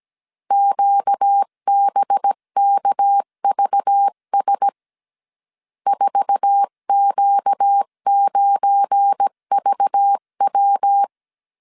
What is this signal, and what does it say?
Morse code, heard as a keyed tone, about 17 wpm, 800 Hz. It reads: QBKVS 4Q9VW
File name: 1500C S B 16wpm